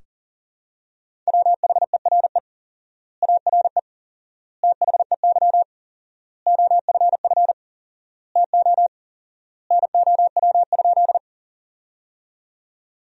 Morse code, heard as a keyed tone, about 40 wpm, 700 Hz. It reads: WHERE ARE THEY OFF TO NOW?
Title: 4x Word-Spacing - Sentences from Top 200 Words - 40wpm